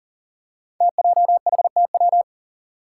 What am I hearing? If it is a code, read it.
Morse code, heard as a keyed tone, about 38 wpm, 700 Hz. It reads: TJHTW
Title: Five Challenging Letter-Number Combinations 40wpm